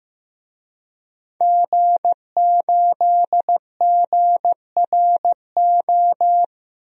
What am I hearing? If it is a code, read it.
Morse code, heard as a keyed tone, about 15 wpm, 700 Hz. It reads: G8GRO
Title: DXCC-100 Call Signs - 5 Characters - 15wpm